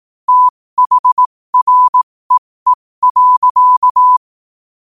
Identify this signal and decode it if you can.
Morse code, heard as a keyed tone, about 18 wpm, spaced as for 15 wpm, 1000 Hz. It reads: THREE.
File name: qotdfiles / 15wpm